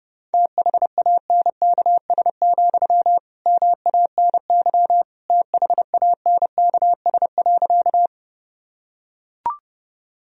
Morse code, heard as a keyed tone, about 30 wpm, 700 Hz. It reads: THANKS, MANY THANKS.
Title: Sentences from Top 1000 Words - Review Part 3 - 30wpm